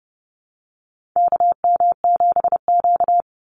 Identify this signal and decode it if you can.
Morse code, heard as a keyed tone, about 30 wpm, 700 Hz. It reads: KM7Q
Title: Call Signs - 30wpm